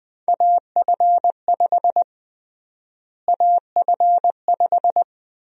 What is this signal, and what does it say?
Morse code, heard as a keyed tone, about 20 wpm, 700 Hz. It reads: AF5 AF5
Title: USA Call Sign Prefixes - Repeated 2x - 20wpm